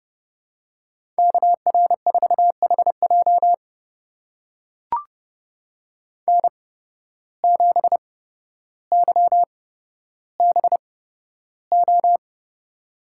Morse code, heard as a keyed tone, about 30 wpm, spaced as for 13 wpm, 700 Hz. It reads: KR4HJ N7YBO
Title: USA Call Signs - 5 Characters - 8x Character Spacing - 30wpm